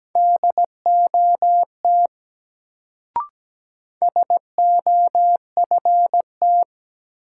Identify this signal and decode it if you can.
Morse code, heard as a keyed tone, about 17 wpm, 700 Hz. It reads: DOT SOFT
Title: Sentences from Top 1500 Words - New Word Drill - 17wpm